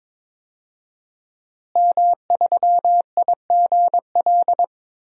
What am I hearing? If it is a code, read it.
Morse code, heard as a keyed tone, about 22 wpm, 700 Hz. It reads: M3IGL
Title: DXCC-100 Call Signs - 5 Characters - Repeated 2x - 22wpm